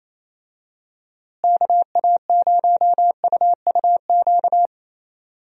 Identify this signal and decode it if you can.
Morse code, heard as a keyed tone, about 28 wpm, 700 Hz. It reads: KA0UUQ
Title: USA Call Signs - 6 Characters - Repeated 3x - 28wpm